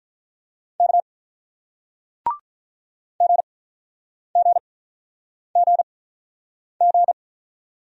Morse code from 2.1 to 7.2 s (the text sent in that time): GGGG